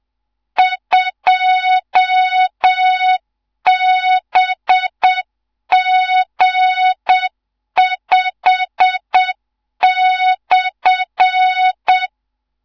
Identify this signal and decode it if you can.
Morse code, heard as a keyed tone, about 7 wpm, 3700 Hz. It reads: 2BG5/
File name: Morze-HA2MN / hang / 15 GYAKORLAT